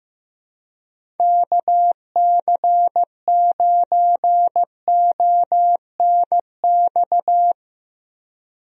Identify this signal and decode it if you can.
Morse code, heard as a keyed tone, about 15 wpm, 700 Hz. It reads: KC9ONX
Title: USA Call Signs - 6 Characters - Repeated 3x - 15wpm